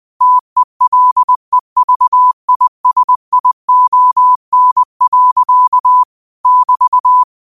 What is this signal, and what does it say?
Morse code, heard as a keyed tone, about 20 wpm, 1000 Hz. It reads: TELEVISION. <BT>
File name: qotdfiles / 20wpm